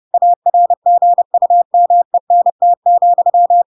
Morse code, heard as a keyed tone, about 30 wpm, 700 Hz. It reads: ARGUMENT,
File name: Pinocchio / Pinocchio ch3 30 30 ik1hge